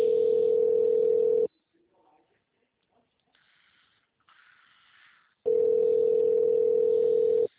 Ringback tone (North America), 440 + 480 Hz.
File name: out- -2007- - - .824